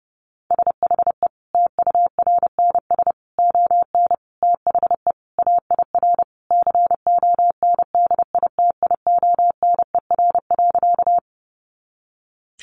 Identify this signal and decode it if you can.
Morse code, heard as a keyed tone, about 28 wpm, 700 Hz. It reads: SHE TURNS ON THE AIR CONDITIONER.
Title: Additional Easy Phrases 30wpm